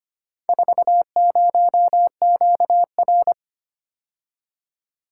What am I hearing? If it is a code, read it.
Morse code, heard as a keyed tone, about 25 wpm, 700 Hz. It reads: 40QR
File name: practice / 25wpm / Four Letter-Number Combinations - 25wpm